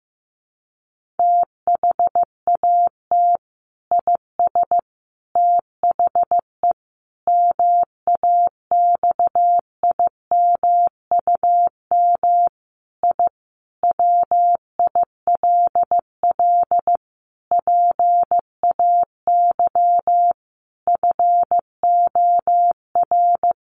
Morse code, heard as a keyed tone, about 15 wpm, 700 Hz. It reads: THAT IS THE MAXIMUM I WILL PAY FOR
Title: Sentences from Top 1300 Words 15wpm